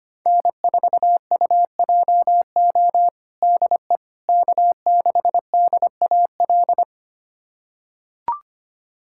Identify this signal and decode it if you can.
Morse code, heard as a keyed tone, about 25 wpm, 700 Hz. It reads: N4UJO DE K6DAL
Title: USA Callsign DE Callsign - 5 Characters - 25wpm